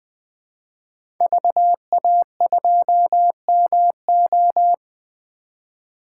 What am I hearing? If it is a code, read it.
Morse code, heard as a keyed tone, about 20 wpm, 700 Hz. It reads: VA2MO
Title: 2023 SOTA Chaser Call Signs - 20wpm